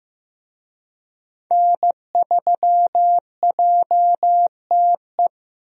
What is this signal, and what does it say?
Morse code, heard as a keyed tone, about 15 wpm, 700 Hz. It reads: N3JTE
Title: USA Call Signs - 5 Characters - Repeated 3x - 15wpm